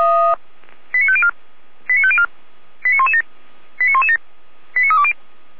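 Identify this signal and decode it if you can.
Five-tone selective call (ZVEI), heard as quick runs of five tones, tones of about 70 ms. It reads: 89504 89504 89117 89117 89320
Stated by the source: Five-tone ZVEI selective call received on an SDR.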